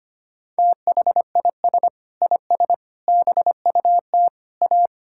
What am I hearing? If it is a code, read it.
Morse code, heard as a keyed tone, about 25 wpm, 700 Hz. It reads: THIS IS BUT A
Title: Sentences from Top 1600 Words 25wpm